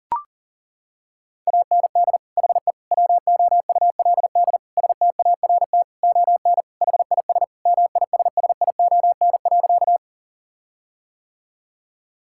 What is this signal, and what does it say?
Morse code, heard as a keyed tone, about 37 wpm, 700 Hz. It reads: AND HE WOULD START ON HIS MISSION.
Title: Sentences from Top 1600 Words 40wpm